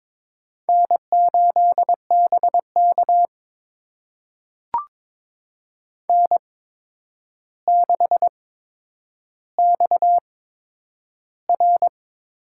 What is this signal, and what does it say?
Morse code, heard as a keyed tone, about 22 wpm, spaced as for 9 wpm, 700 Hz. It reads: N8BK N6XR
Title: USA Call Signs - 4 Characters - 8x Character Spacing - 22wpm